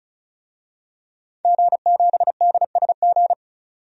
Morse code, heard as a keyed tone, about 34 wpm, 700 Hz. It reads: G7DSG
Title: DXCC-100 Call Signs - 5 Characters - 35wpm